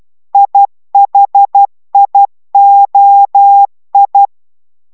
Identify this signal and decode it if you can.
Morse code, heard as a keyed tone, about 12 wpm, 800 Hz. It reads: IHIOI